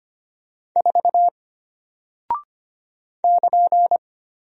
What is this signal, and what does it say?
Morse code, heard as a keyed tone, about 25 wpm, 700 Hz. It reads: 4 <KN>
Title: Single Character (Letter:Number:Punctuation:Prosign) Alt 25wpm